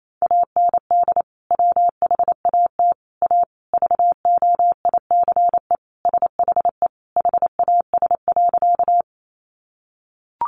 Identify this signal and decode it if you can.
Morse code, heard as a keyed tone, about 28 wpm, 700 Hz. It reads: AND WHAT A VOICE SHE HAS.
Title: Sentences from Top 500 Words 28wpm